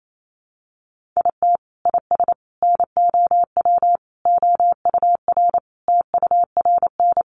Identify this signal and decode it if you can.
Morse code, heard as a keyed tone, about 28 wpm, 700 Hz. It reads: IT IS NOW OUR TURN
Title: Sentences from Top 1400 Words 28wpm